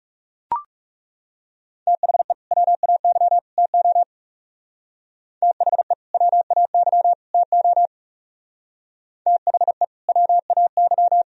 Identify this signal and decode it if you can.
Morse code, heard as a keyed tone, about 37 wpm, 700 Hz. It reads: THE WAY TO THE WAY TO THE WAY
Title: Speed-Racing - Sets of 3 Words 30wpm